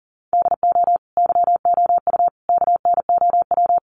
Morse code, heard as a keyed tone, about 40 wpm, 700 Hz. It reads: DO YOU KNOW
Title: Sentences from Top 100 Words 40wpm